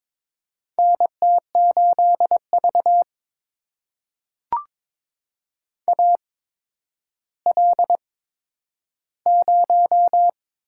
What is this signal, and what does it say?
Morse code, heard as a keyed tone, about 22 wpm, spaced as for 11 wpm, 700 Hz. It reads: NT8V AL0
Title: USA Call Signs - 4 Characters - 8x Character Spacing - 22wpm